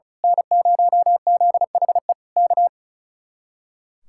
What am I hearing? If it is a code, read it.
Morse code, heard as a keyed tone, about 33 wpm, 700 Hz. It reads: N0ZHE K
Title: Speed-Racing Common QSO Phrases - 35wpm